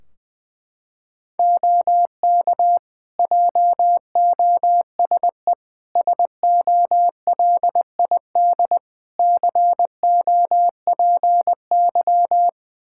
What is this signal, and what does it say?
Morse code, heard as a keyed tone, about 20 wpm, 700 Hz. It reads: OK JOSE SOLID COPY
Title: Practice QSO - Exchange 3 - More Info - 20wpm